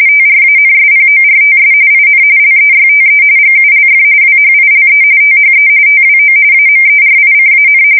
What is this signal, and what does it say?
RTTY, about 45.45 baud. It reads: TRANSPORTS IN LEYTE GULF WHILE THE TWO AMERICAN
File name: fsk170 / battleshipnj